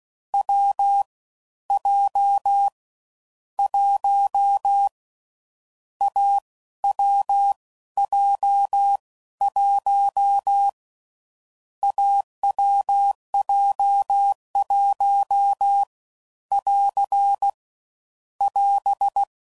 Morse code, heard as a keyed tone, about 16 wpm, spaced as for 13 wpm, 800 Hz. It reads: W J 1 AWJ1 AWJ1 <AR> <AS>
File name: Combinations / Difficult Combos / AWJ1